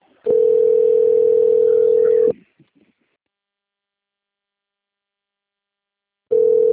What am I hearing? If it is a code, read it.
Ringback tone (North America), 440 + 480 Hz.